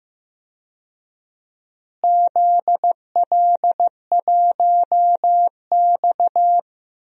Morse code, heard as a keyed tone, about 15 wpm, 700 Hz. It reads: ZL1X
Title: DXCC-100 Call Signs - 4 Characters - Repeated 3x - 15wpm